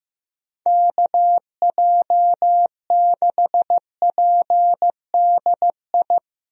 Morse code, heard as a keyed tone, about 15 wpm, 700 Hz. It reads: KJ6PDI
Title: USA Call Signs - 6 Characters - Repeated 3x - 15wpm